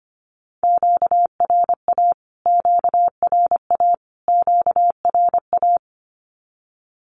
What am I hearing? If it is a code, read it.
Morse code, heard as a keyed tone, about 25 wpm, 700 Hz. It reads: QRA QRA QRA
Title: Q-Codes Repeated 3x 25wpm